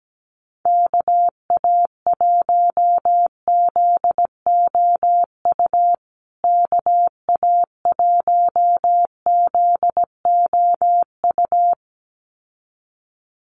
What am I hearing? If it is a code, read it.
Morse code, heard as a keyed tone, about 17 wpm, 700 Hz. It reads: KA1ZOU KA1ZOU
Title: USA Call Signs Repeated 2x 17wpm